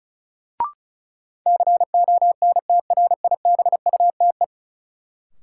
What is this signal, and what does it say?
Morse code, heard as a keyed tone, about 34 wpm, 700 Hz. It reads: CONTRIBUTE
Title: Sentences from Top 1400 Words - New Word Drill - 35wpm